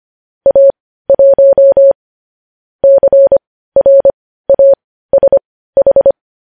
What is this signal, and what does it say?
Morse code, heard as a keyed tone, about 25 wpm, spaced as for 15 wpm, 550 Hz. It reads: A1 CRASH
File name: podcast / News headlines-1x-25x15